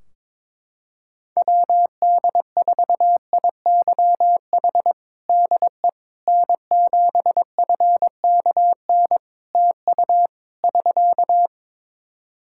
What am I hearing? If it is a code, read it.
Morse code, heard as a keyed tone, about 22 wpm, 700 Hz. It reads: WD4IYH DE N7FKN TU <SK>
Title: Practice QSO - Exchange 4 - Ending - 22wpm